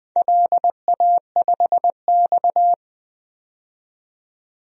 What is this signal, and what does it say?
Morse code, heard as a keyed tone, about 20 wpm, 700 Hz. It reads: LA5X
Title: DXCC-100 Call Signs - 4 Characters - 20wpm